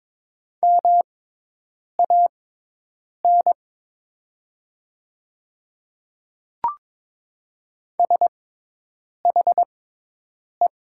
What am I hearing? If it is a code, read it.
Morse code, heard as a keyed tone, about 22 wpm, spaced as for 8 wpm, 700 Hz. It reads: MAN SHE
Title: 6x Character Spacing - Top 200 Words - 22wpm